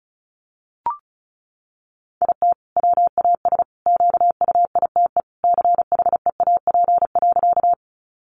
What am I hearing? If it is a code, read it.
Morse code, heard as a keyed tone, about 34 wpm, 700 Hz. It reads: IT WAS QUITE CHEAP.
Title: Sentences from Top 1200 Words 35wpm